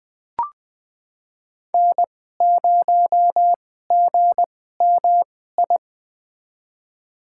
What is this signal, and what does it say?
Morse code, heard as a keyed tone, about 20 wpm, spaced as for 14 wpm, 700 Hz. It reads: N0GMI
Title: USA Call Signs - 5 Characters - 2x Character Spacing - 20wpm